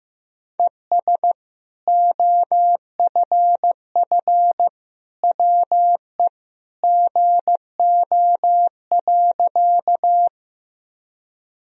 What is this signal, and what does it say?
Morse code, heard as a keyed tone, about 15 wpm, 700 Hz. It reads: ES OFF WE GO.
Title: Sentences from Top 200 Words and Abbreviations 15wpm